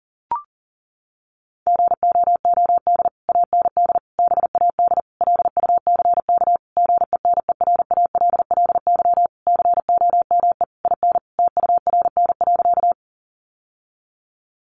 Morse code, heard as a keyed tone, about 37 wpm, 700 Hz. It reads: GOOD AND BAD LUCK GENERALLY COME IN TURN.
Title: Sentences from Top 1500 Words 40wpm